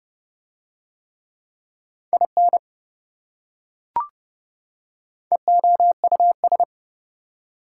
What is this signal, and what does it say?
Morse code, heard as a keyed tone, about 28 wpm, 700 Hz. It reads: IN EOUS